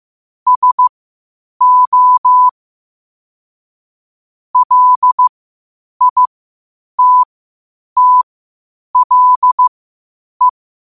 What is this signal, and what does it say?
Morse code, heard as a keyed tone, about 15 wpm, spaced as for 8 wpm, 1000 Hz. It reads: SO LITTLE